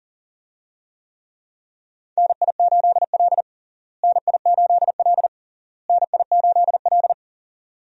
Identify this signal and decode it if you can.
Morse code, heard as a keyed tone, about 37 wpm, 700 Hz. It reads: NI8L NI8L NI8L
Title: USA Call Signs - 4 Characters - Repeated 3x - 40wpm